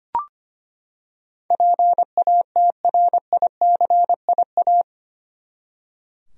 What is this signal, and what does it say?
Morse code, heard as a keyed tone, about 25 wpm, 700 Hz. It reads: PATRICIA
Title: USA Common Names - 25wpm